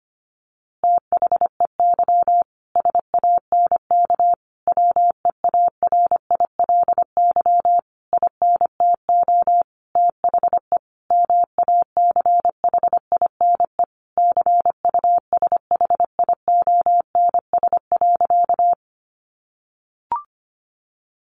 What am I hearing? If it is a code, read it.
Morse code, heard as a keyed tone, about 25 wpm, 700 Hz. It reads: THEY SANK WEARILY INTO THE MACHINE CUSHIONS.
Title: A Sound Of Thunder 25wpm